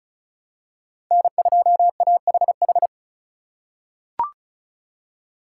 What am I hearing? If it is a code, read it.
Morse code, heard as a keyed tone, about 36 wpm, 700 Hz. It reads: N2AHH
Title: USA Call Signs - 5 Characters - 2x Character Spacing - 35wpm